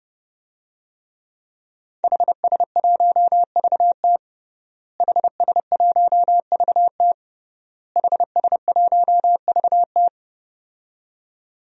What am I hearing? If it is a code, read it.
Morse code, heard as a keyed tone, about 30 wpm, 700 Hz. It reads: HS1VT HS1VT HS1VT
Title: DXCC-100 Call Signs - 5 Characters - Repeated 3x - 30wpm